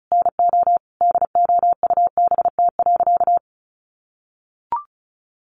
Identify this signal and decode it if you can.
Morse code, heard as a keyed tone, about 34 wpm, 700 Hz. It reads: NO DOUBT.
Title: Sentences from Top 1500 Words 35wpm